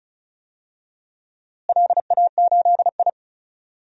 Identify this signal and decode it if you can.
Morse code, heard as a keyed tone, about 34 wpm, 700 Hz. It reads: LA8I
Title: DXCC-100 Call Signs - 4 Characters - Repeated 2x - 35wpm